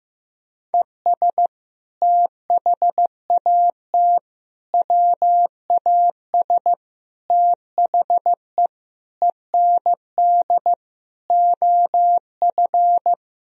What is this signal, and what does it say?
Morse code, heard as a keyed tone, about 15 wpm, 700 Hz. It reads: ES THAT WAS THE END OF
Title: Sentences from Top 200 Words and Abbreviations 15wpm